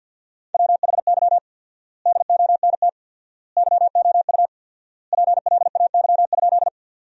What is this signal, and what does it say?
Morse code, heard as a keyed tone, about 53 wpm, 700 Hz. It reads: WHY DONT YOU PLAY?
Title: 4x Word-Spacing - Sentences from Top 300 Words - 50wpm